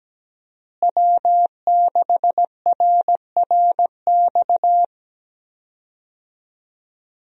Morse code, heard as a keyed tone, about 17 wpm, 700 Hz. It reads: W6RRX
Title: USA Call Signs - 5 Characters - Repeated 2x - 17wpm